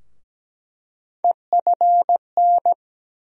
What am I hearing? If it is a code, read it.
Morse code, heard as a keyed tone, about 17 wpm, 700 Hz. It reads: EFN
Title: USA Call Sign Postfixes - 3 Letters 17wpm